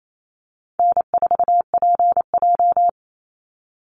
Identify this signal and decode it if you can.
Morse code, heard as a keyed tone, about 28 wpm, 700 Hz. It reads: N4PJ
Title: Easy Call Signs Repeated 2x - 28wpm